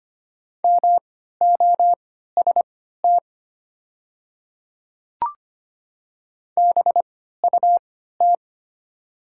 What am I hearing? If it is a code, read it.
Morse code, heard as a keyed tone, about 25 wpm, spaced as for 14 wpm, 700 Hz. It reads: MOST BUT
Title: 3x Character Spacing - Top 100 Words - 25wpm